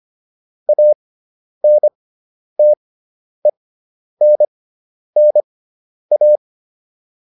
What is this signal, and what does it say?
Morse code, heard as a keyed tone, about 25 wpm, spaced as for 10 wpm, 600 Hz. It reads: ANTENNA